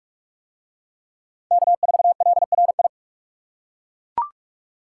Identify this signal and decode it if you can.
Morse code, heard as a keyed tone, about 41 wpm, 700 Hz. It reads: K4LRI